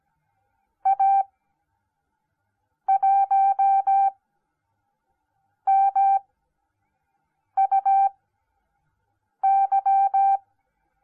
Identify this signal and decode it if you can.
Morse code, heard as a keyed tone, about 17 wpm, spaced as for 5 wpm, 780 Hz. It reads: A1MUY